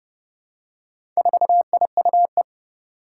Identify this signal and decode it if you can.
Morse code, heard as a keyed tone, about 30 wpm, 700 Hz. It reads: 4IUE
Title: Four Challenging Letter-Number Combinations 30wpm